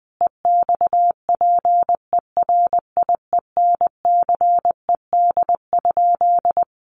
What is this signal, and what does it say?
Morse code, heard as a keyed tone, about 19 wpm, 700 Hz. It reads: EXPERIENCED?
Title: Sentences from Top 1100 Words 20wpm